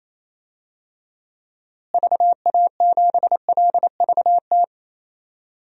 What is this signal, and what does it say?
Morse code, heard as a keyed tone, about 28 wpm, 700 Hz. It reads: VA7LVT
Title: DXCC-100 Call Signs - 6 Characters - Repeated 3x - 28wpm